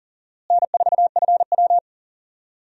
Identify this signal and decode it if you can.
Morse code, heard as a keyed tone, about 40 wpm, 700 Hz. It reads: N4FW
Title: USA Call Signs - 4 Characters - Repeated 3x - 40wpm